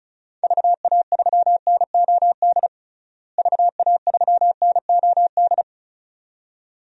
Morse code, heard as a keyed tone, about 34 wpm, 700 Hz. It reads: VA3NOD VA3NOD
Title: DXCC-100 Call Signs - 6 Characters - Repeated 2x - 35wpm